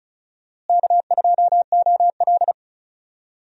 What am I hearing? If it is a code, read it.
Morse code, heard as a keyed tone, about 36 wpm, 700 Hz. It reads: K2OL
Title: USA Call Signs - 4 Characters - Repeated 3x - 35wpm